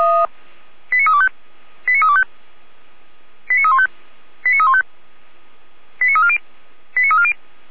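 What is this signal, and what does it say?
Five-tone selective call (ZVEI), heard as quick runs of five tones, tones of about 70 ms. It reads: 89325 89325 89315 89315 89340 89340
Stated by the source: Five-tone ZVEI selective call received by SDR, from the Feuerwehr Orlamünde fire brigade audio archive.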